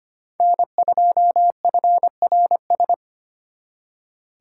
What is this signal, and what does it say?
Morse code, heard as a keyed tone, about 32 wpm, 700 Hz. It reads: N2FRS